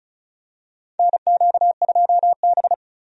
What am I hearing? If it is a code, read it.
Morse code, heard as a keyed tone, about 36 wpm, 700 Hz. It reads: NQ2B
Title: USA Call Signs - 4 Characters - 35wpm